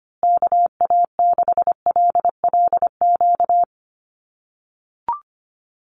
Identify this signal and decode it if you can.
Morse code, heard as a keyed tone, about 25 wpm, 700 Hz. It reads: KA6LLQ
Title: USA Call Signs Repeated 3x 25wpm